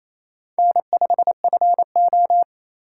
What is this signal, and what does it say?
Morse code, heard as a keyed tone, about 28 wpm, 700 Hz. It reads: N5FO
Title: USA Call Signs - 4 Characters - 2x Character Spacing - 28wpm